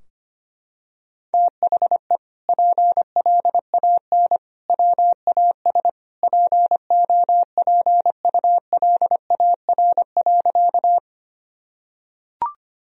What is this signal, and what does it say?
Morse code, heard as a keyed tone, about 25 wpm, 700 Hz. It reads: THE PLAN WAS POPULAR.
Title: Sentences from Top 1000 Words - Review Part 3 - 25wpm